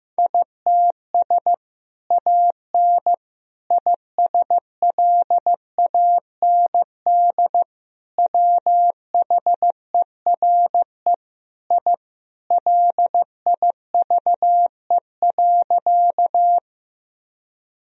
Morse code, heard as a keyed tone, about 16 wpm, 700 Hz. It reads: ITS AN ISLAND WHERE I LIVE.
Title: Sentences from Top 1000 Words - Review Part 3 - 15wpm